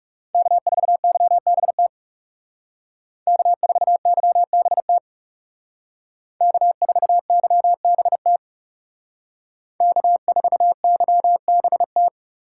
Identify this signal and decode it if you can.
Morse code, heard as a keyed tone, about 40 wpm, 700 Hz. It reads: K4YBT K4YBT K4YBT K4YBT